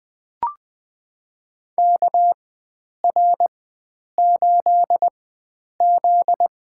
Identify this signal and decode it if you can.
Morse code, heard as a keyed tone, about 20 wpm, spaced as for 9 wpm, 700 Hz. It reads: KR8Z
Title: USA Call Signs - 4x Character Spacing - 20wpm